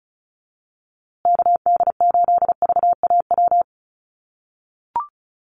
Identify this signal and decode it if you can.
Morse code, heard as a keyed tone, about 34 wpm, 700 Hz. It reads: KD8VAW
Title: USA Call Signs Repeated 3x 35wpm